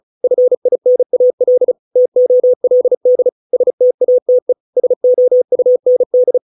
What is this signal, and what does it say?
Morse code, heard as a keyed tone, about 34 wpm, 500 Hz. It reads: FINAL TOLD STATE SOUND